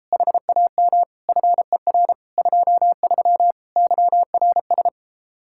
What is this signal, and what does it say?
Morse code, heard as a keyed tone, about 33 wpm, 700 Hz. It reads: HAM FER 23 YRS